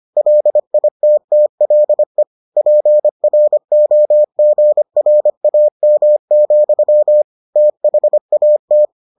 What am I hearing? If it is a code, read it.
Morse code, heard as a keyed tone, about 25 wpm, 600 Hz. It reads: LITTLE PROGRAM, THAT